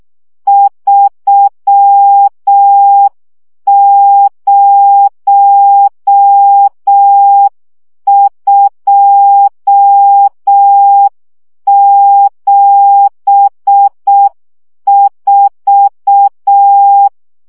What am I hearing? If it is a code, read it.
Morse code, heard as a keyed tone, about 6 wpm, 800 Hz. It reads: 30274
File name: morze vizsga 2011-2 5